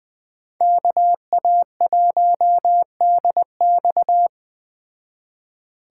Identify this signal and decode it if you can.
Morse code, heard as a keyed tone, about 20 wpm, 700 Hz. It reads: KA1DX
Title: Speed Racing - USA 2x2 Call Signs 17wpm